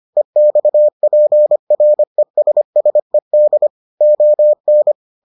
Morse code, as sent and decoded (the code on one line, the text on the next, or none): . -..- .--. .-. ... ... -.. / --- -.
EXPRESSED ON